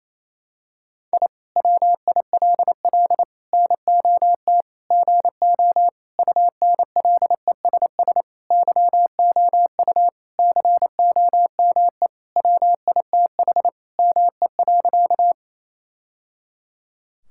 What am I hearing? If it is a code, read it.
Morse code, heard as a keyed tone, about 28 wpm, 700 Hz. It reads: I WILL NOT GO UNLESS YOU COME WITH ME.
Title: Sentences from Top 1000 Words - Review Part 6 - 28wpm